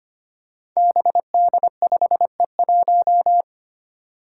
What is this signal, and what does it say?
Morse code, heard as a keyed tone, about 25 wpm, 700 Hz. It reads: BD5E1
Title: Five Challenging Letter-Number Combinations 25wpm